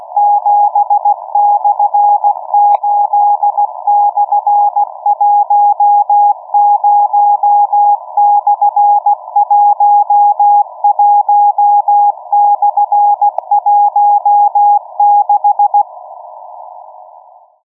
Morse code, heard as a keyed tone, about 16 wpm, 820 Hz. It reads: 7/8/10/11/16
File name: europe / d-3086-bundesmarine wilhelmshaven'g23b-dhj59'-cw-(1986)-(rb)-(mc-913-2-25)